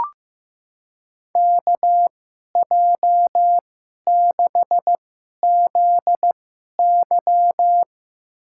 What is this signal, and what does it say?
Morse code, heard as a keyed tone, about 15 wpm, spaced as for 11 wpm, 700 Hz. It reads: KJ6ZY
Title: USA Call Signs - 5 Characters - 2x Character Spacing - 15wpm